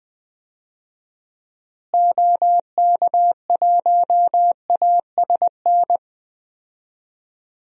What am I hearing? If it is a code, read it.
Morse code, heard as a keyed tone, about 20 wpm, 700 Hz. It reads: OK1ASN
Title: DXCC-100 Call Signs - 6 Characters - Repeated 3x - 20wpm